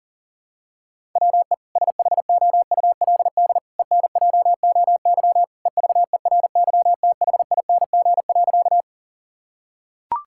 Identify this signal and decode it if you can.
Morse code, heard as a keyed tone, about 40 wpm, 700 Hz. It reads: WE SHOULD ENJOY EVERYTHING.
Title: Sentences from Top 1000 Words - Review Part 5 - 40wpm